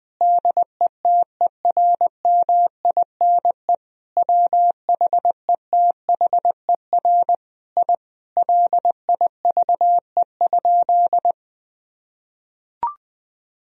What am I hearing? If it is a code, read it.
Morse code, heard as a keyed tone, about 20 wpm, 700 Hz. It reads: DETERMINE WHETHER I LIVE?
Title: Sentences from Top 1000 Words - Review Part 4 - 20wpm